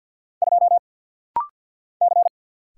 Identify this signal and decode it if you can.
Morse code, heard as a keyed tone, about 48 wpm, 700 Hz. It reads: 2 C
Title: Single Letter-Number - Rapid-Fire 50wpm